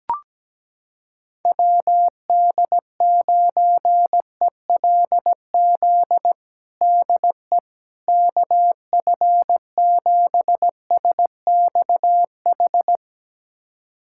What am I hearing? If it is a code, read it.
Morse code, heard as a keyed tone, about 17 wpm, 700 Hz. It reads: WD9ELZ DE KF7SXH
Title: USA Callsign DE Callsign - 6 Characters - 17wpm